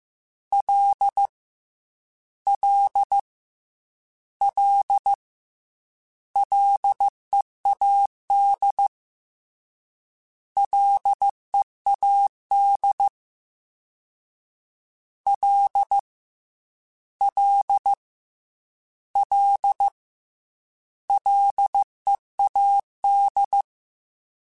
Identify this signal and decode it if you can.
Morse code, heard as a keyed tone, about 15 wpm, 800 Hz. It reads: L L L LEAD LEAD L L L LEAD
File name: Ls / Single L m3s / Single L @13